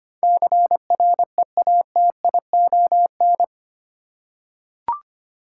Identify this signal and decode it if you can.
Morse code, heard as a keyed tone, about 25 wpm, 700 Hz. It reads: CREATION
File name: 8 Letter Words - First 3 Letters Given - 25wpm